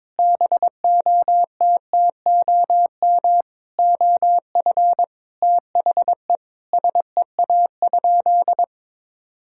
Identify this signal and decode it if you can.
Morse code, heard as a keyed tone, about 22 wpm, 700 Hz. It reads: BOTTOM OF THE SEA?
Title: Sentences from Top 1000 Words - Review Part 6 - 22wpm